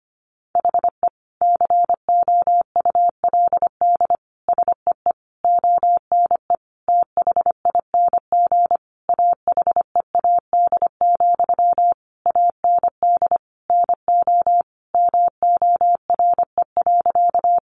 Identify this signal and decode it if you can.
Morse code, heard as a keyed tone, about 25 wpm, 700 Hz. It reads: HE COULD SEE ONE THING AHEAD, AND NO MORE.
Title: Sentences from Top 1100 Words 25wpm